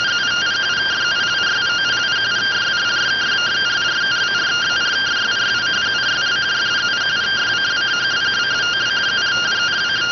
Two-tone teleprinter signal, about 170 Hz shift.